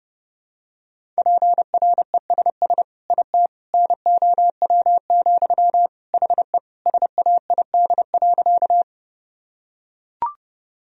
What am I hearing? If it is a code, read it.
Morse code, heard as a keyed tone, about 30 wpm, 700 Hz. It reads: PRESS IT NOW, HE SAID.
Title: Sentences from Top 1000 Words - Review Part 1 - 30wpm